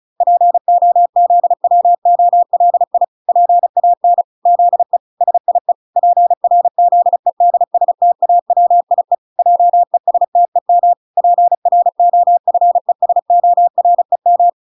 Morse code, heard as a keyed tone, about 34 wpm, 700 Hz. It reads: POZWOLI PAN ZE SIE PRZEDSTAWIE JESTEM PROFESOREM